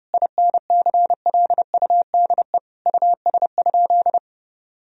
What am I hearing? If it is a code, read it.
Morse code, heard as a keyed tone, about 30 wpm, 700 Hz. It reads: INCLUDE US?
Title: Sentences from Top 1000 Words - Review Part 3 - 30wpm